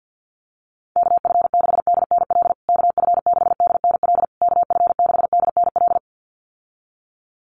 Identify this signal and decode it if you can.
Morse code, heard as a keyed tone, about 50 wpm, 700 Hz. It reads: KF6DNL KF6DNL KF6DNL
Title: USA Call Signs Repeated 3x 50wpm